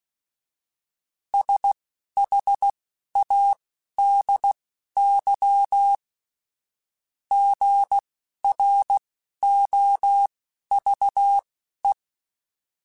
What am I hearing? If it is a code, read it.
Morse code, heard as a keyed tone, about 16 wpm, spaced as for 11 wpm, 800 Hz. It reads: SHADY GROVE